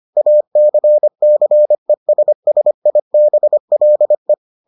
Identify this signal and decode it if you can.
Morse code, heard as a keyed tone, about 25 wpm, 600 Hz. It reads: ACCESSIBLE